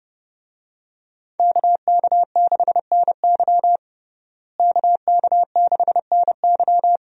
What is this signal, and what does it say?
Morse code, heard as a keyed tone, about 30 wpm, 700 Hz. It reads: KK6NY KK6NY
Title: USA Call Signs - 5 Characters - Repeated 2x - 30wpm